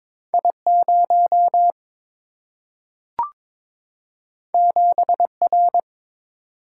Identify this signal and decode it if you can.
Morse code, heard as a keyed tone, about 22 wpm, 700 Hz. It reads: I0 7R
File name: practice / 22wpm / Two Letter-Number Combinations - 22wpm